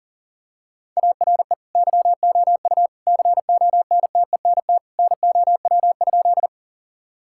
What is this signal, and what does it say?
Morse code, heard as a keyed tone, about 40 wpm, 700 Hz. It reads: ARE YOU CONTENT NOW?
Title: Sentences from Top 1000 Words - Review Part 6 - 40wpm